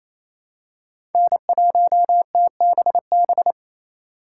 Morse code, heard as a keyed tone, about 28 wpm, 700 Hz. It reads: N1TBB
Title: USA Call Signs - 5 Characters - 2x Character Spacing - 28wpm